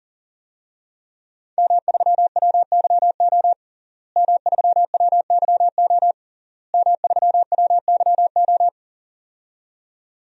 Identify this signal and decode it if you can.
Morse code, heard as a keyed tone, about 40 wpm, 700 Hz. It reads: M3WYO M3WYO M3WYO
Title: DXCC-100 Call Signs - 5 Characters - Repeated 3x - 40wpm